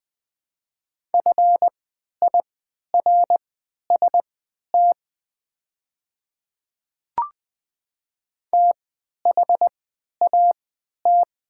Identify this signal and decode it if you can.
Morse code, heard as a keyed tone, about 20 wpm, spaced as for 11 wpm, 700 Hz. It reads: FIRST THAT
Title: 3x Character Spacing - Top 200 Words - 20wpm